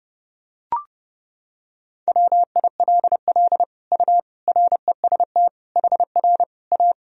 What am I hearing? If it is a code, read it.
Morse code, heard as a keyed tone, about 30 wpm, 700 Hz. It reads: WILL U REST HR A
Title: Sentences from Top 500 Words and Abbreviations 30wpm